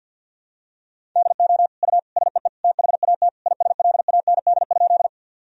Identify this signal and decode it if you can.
Morse code, heard as a keyed tone, about 50 wpm, 700 Hz. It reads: DO U SEE THAT ISLAND?
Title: Sentences from Top 500 Words and Abbreviations 50wpm